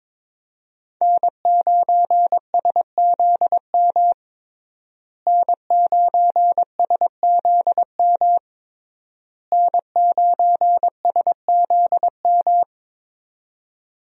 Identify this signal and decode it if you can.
Morse code, heard as a keyed tone, about 22 wpm, 700 Hz. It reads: N9SZM N9SZM N9SZM
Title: USA Call Signs - 5 Characters - Repeated 3x - 22wpm